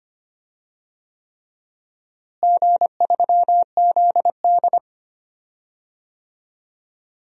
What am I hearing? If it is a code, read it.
Morse code, heard as a keyed tone, about 25 wpm, 700 Hz. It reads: G3ZD
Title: DXCC-100 Call Signs - 4 Characters - Repeated 3x - 25wpm